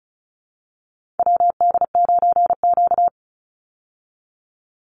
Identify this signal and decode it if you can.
Morse code, heard as a keyed tone, about 34 wpm, 700 Hz. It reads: WD9Q
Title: Call Signs Repeated 3x - 35wpm